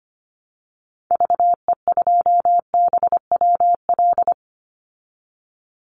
Morse code, heard as a keyed tone, about 25 wpm, 700 Hz. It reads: VE2BWL
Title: Call Signs Repeated 3x - 25wpm